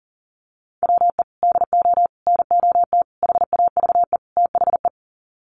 Morse code, heard as a keyed tone, about 40 wpm, 700 Hz. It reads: WE DO NOT HAVE THE
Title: Sets of 5 Words 40wpm